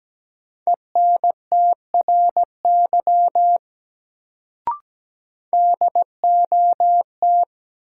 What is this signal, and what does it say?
Morse code, heard as a keyed tone, about 17 wpm, 700 Hz. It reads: ENTRY DOT
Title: Sentences from Top 1500 Words - New Word Drill Alt - 17wpm